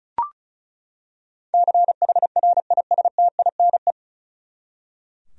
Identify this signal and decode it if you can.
Morse code, heard as a keyed tone, about 36 wpm, 700 Hz. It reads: CHRISTINE
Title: USA Common Names - 35wpm